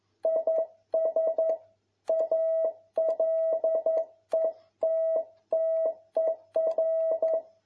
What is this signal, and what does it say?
Morse code, heard as a keyed tone, about 11 wpm, 650 Hz. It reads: IS A LETTER
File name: jh8cbh / 2021cw